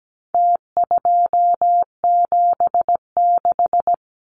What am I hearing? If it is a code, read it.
Morse code, heard as a keyed tone, about 17 wpm, 700 Hz. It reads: T276
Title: Numbers - Contest Style - 4-Digit 17wpm